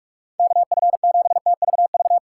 Morse code, heard as a keyed tone, about 46 wpm, 700 Hz. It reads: KF7TVV
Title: USA Call Signs - 3x Character Spacing - 45wpm